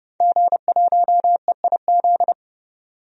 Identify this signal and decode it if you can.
Morse code, heard as a keyed tone, about 30 wpm, 700 Hz. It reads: G1EIZ
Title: DXCC-100 Call Signs - 5 Characters - 30wpm